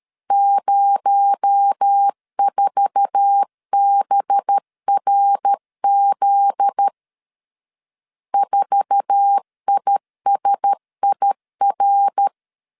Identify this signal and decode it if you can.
Morse code, heard as a keyed tone, about 13 wpm, 800 Hz. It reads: 04BRZ 4ISIR
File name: LEZIONI / 8 YJ45 / 8 YJ4 12wpm 2151